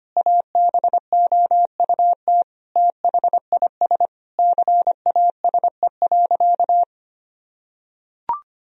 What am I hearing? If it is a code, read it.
Morse code, heard as a keyed tone, about 25 wpm, 700 Hz. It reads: ABOUT THIS CASE.
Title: Sentences from Top 1000 Words - Review Part 4 - 25wpm